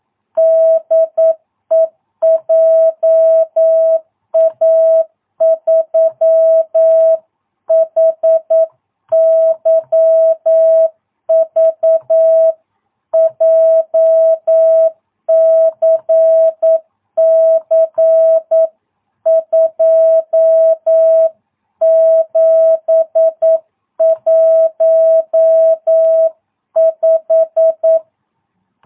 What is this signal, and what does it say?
Morse code, heard as a keyed tone, about 9 wpm, 660 Hz. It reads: DE JA3HYV JCC 2715
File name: jh8cbh / 2022cw